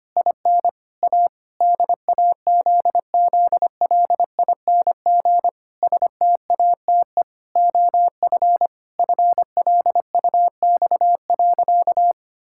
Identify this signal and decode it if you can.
Morse code, heard as a keyed tone, about 25 wpm, 700 Hz. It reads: IN A DAZZLING STATE OF FLUX.
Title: Pangram Challenge v2 - 25wpm